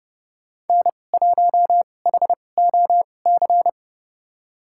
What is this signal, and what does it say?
Morse code, heard as a keyed tone, about 31 wpm, spaced as for 22 wpm, 700 Hz. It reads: N1HOC
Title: USA Call Signs - 5 Characters - 2x Character Spacing - 30wpm